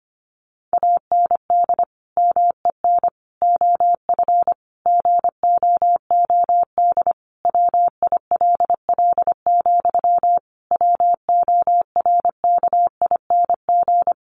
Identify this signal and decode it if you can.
Morse code, heard as a keyed tone, about 25 wpm, 700 Hz. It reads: AND MEN OF GOOD WILL, WORKING
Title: Sentences from Top 1400 Words 25wpm